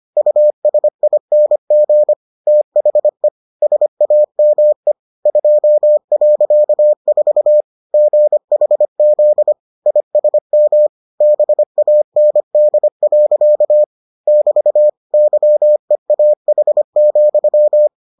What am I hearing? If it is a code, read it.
Morse code, heard as a keyed tone, about 25 wpm, 600 Hz. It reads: USING THE SAME 2.4 GHZ ISM BAND. <BT> YEAH,